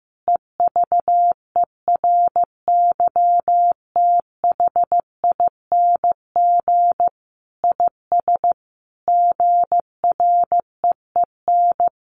Morse code, heard as a keyed tone, about 15 wpm, 700 Hz. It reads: EVERYTHING IS GREEN
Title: Sentences from Top 700 Words 15wpm